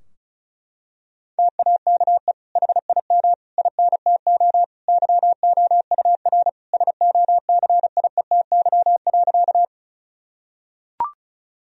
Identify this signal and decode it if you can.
Morse code, heard as a keyed tone, about 34 wpm, 700 Hz. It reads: TAKE HIM INTO YOUR SOCIETY.
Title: Sentences from Top 1000 Words - Review Part 4 - 35wpm